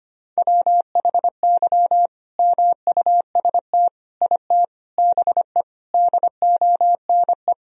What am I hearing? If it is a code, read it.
Morse code, heard as a keyed tone, about 25 wpm, 700 Hz. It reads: WHY MUST IT BE DONE
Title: Sentences from Top 1000 Words - Review Part 3 - 25wpm